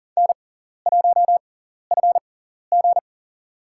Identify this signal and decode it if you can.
Morse code, heard as a keyed tone, about 40 wpm, spaced as for 14 wpm, 700 Hz. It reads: N1FG
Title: USA Call Signs - 6x Character Spacing - 40wpm